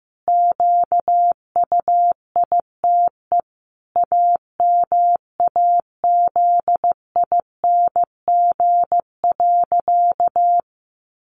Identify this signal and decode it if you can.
Morse code, heard as a keyed tone, about 15 wpm, 700 Hz. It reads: QUITE AMAZING.
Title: Sentences from Top 1500 Words 15wpm